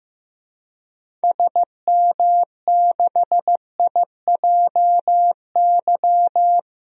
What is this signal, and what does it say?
Morse code, heard as a keyed tone, about 16 wpm, 700 Hz. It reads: SM6IJY
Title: DXCC-100 Call Signs - 6 Characters - Repeated 2x - 15wpm